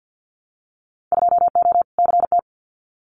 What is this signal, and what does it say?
Morse code, heard as a keyed tone, about 50 wpm, 700 Hz. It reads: 2O CT
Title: Winter Field Day Exchanges 50wpm